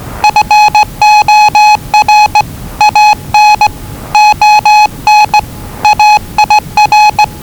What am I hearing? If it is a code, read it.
Morse code, heard as a keyed tone, about 20 wpm, 840 Hz. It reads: FOR AN ON AIR
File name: WellingtonRadioZlwCloseDownOnHfAndMf / 01-Zlw 4277khz 417.5 qrt until 0925z